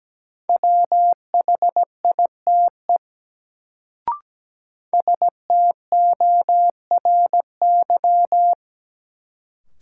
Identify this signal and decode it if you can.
Morse code, heard as a keyed tone, about 17 wpm, 700 Hz. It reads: WHITE STORY
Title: Sentences from Top 300 Words - New Word Drill - 17wpm